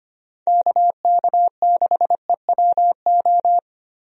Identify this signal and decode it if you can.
Morse code, heard as a keyed tone, about 25 wpm, 700 Hz. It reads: KK6EWO
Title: USA Call Signs - 6 Characters - Repeated 2x - 25wpm